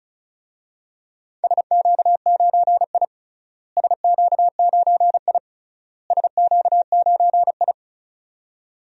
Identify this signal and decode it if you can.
Morse code, heard as a keyed tone, about 34 wpm, 700 Hz. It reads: SQ9I SQ9I SQ9I
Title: DXCC-100 Call Signs - 4 Characters - Repeated 3x - 35wpm